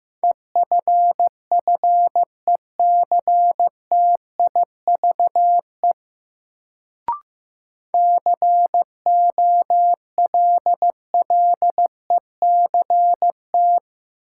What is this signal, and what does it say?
Morse code, heard as a keyed tone, about 15 wpm, 700 Hz. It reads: EFFECTIVE COLLECT
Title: Sentences from Top 1600 Words - New Word Drill Alt - 15wpm